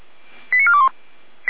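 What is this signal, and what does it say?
Five-tone selective call (ZVEI), heard as quick runs of five tones, tones of about 70 ms. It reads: 89421
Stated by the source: ZVEI five-tone call received by SDR, logged by Feuerwehr Orlamünde.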